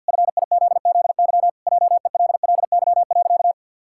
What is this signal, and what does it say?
Morse code, heard as a keyed tone, about 53 wpm, 700 Hz. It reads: FIZZY JELLY.
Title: Pangram Challenge v2 - 50wpm